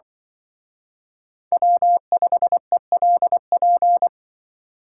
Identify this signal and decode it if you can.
Morse code, heard as a keyed tone, about 24 wpm, 700 Hz. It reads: W5ELP
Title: Speed Racing - USA 1x3 Call Signs 20wpm